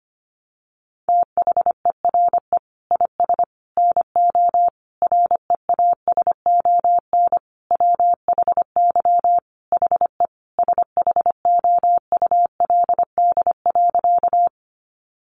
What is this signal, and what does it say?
Morse code, heard as a keyed tone, about 25 wpm, 700 Hz. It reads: THERE IS NO REASON WHY HE SHOULD.
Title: Sentences from Top 700 Words 25wpm